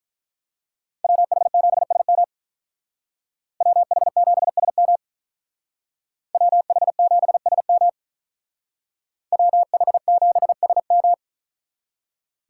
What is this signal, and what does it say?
Morse code, heard as a keyed tone, about 43 wpm, 700 Hz. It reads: WH7SM WH7SM WH7SM WH7SM
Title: Speed Racing - USA 2x2 Call Signs 35wpm